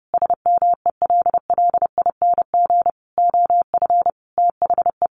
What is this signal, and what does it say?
Morse code, heard as a keyed tone, about 30 wpm, 700 Hz. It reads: SMELLING OF THE
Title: A Sound Of Thunder 30wpm